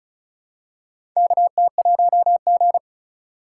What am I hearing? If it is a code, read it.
Morse code, heard as a keyed tone, about 36 wpm, 700 Hz. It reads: KT1G
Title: USA Call Signs - 4 Characters - 4x Character Spacing - 35wpm